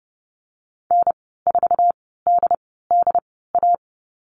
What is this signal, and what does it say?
Morse code, heard as a keyed tone, about 30 wpm, spaced as for 17 wpm, 700 Hz. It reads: N4DDA
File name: Lesson 14g - Number 2 - Call Signs 30wpm